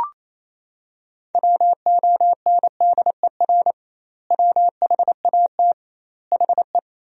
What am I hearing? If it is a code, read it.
Morse code, heard as a keyed tone, about 28 wpm, 700 Hz. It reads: WONDER WHAT HE
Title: 2x Word-Spacing - Sentences from Top 500 Words - 28wpm